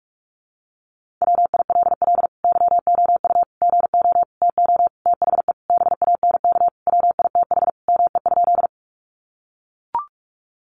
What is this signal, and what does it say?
Morse code, heard as a keyed tone, about 44 wpm, 700 Hz. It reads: WILL YOU GO TO THE BANK WITH ME?
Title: Sentences from Top 900 Words 45wpm